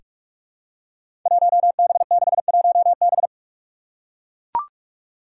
Morse code, heard as a keyed tone, about 41 wpm, 700 Hz. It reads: 1B61B
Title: Five Challenging Letter-Number Combinations 45wpm